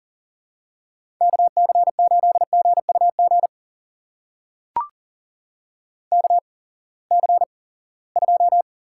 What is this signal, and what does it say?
Morse code, heard as a keyed tone, about 40 wpm, spaced as for 23 wpm, 700 Hz. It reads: KC8GUG KC2
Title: USA Call Signs - 6 Characters - 8x Character Spacing - 40wpm